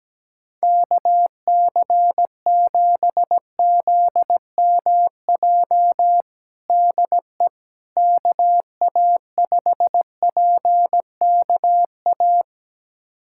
Morse code, heard as a keyed tone, about 17 wpm, 700 Hz. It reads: KC7ZMJ DE KA5PKA
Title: USA Callsign DE Callsign - 6 Characters - 17wpm